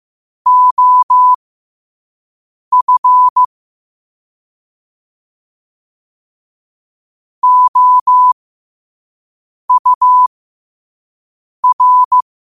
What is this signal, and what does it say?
Morse code, heard as a keyed tone, about 15 wpm, spaced as for 5 wpm, 1000 Hz. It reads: OF OUR
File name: qotdfiles / 5wpm